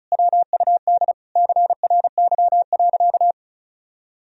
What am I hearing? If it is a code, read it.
Morse code, heard as a keyed tone, about 36 wpm, 700 Hz. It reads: WUD CRY.
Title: Sentences from Top 500 Words and Abbreviations 35wpm